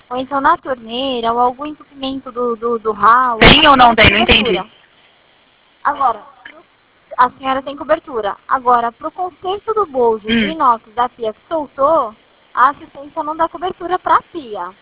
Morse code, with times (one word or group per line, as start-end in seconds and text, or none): none